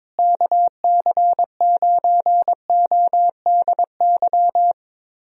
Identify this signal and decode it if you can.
Morse code, heard as a keyed tone, about 22 wpm, 700 Hz. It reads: KC9ODY
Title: USA Call Signs - 6 Characters - Repeated 3x - 22wpm